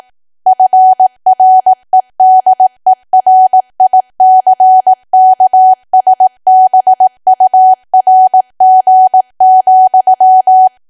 Morse code, heard as a keyed tone, about 18 wpm, 750 Hz. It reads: FREDERICKSBURG,
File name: cwa / W1AW-Level2 / 18WPM